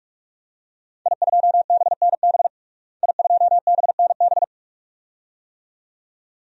Morse code, heard as a keyed tone, about 46 wpm, 700 Hz. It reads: I2BNB I2BNB
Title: DXCC-100 Call Signs - 5 Characters - Repeated 2x - 45wpm